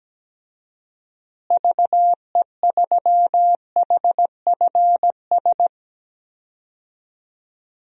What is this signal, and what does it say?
Morse code, heard as a keyed tone, about 17 wpm, 700 Hz. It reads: VE3HFS
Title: DXCC-100 Call Signs - 6 Characters - Repeated 3x - 17wpm